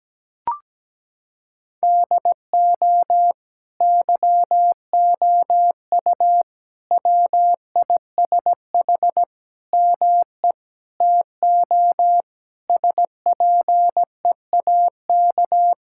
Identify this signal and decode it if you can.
Morse code, heard as a keyed tone, about 17 wpm, 700 Hz. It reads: DO YOU WISH ME TO SPEAK
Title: Sentences from Top 1600 Words 17wpm